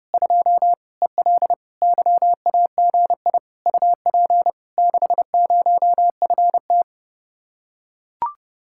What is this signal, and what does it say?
Morse code, heard as a keyed tone, about 32 wpm, 700 Hz. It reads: 2 EL YAGI UP 60FT